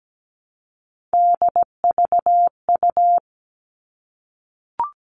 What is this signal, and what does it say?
Morse code, heard as a keyed tone, about 17 wpm, 700 Hz. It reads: DVU
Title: All Cut Numbers - 3-Digit 17wpm